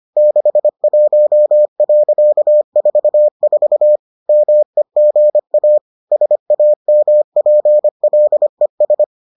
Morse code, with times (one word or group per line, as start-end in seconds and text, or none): 0.2-4.0 61.44
4.3-5.8 MEGA
6.1-9.0 SAMPLES